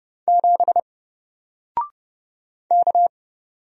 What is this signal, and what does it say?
Morse code, heard as a keyed tone, about 30 wpm, 700 Hz. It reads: 7 K
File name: Single-Letter-Number-30wpm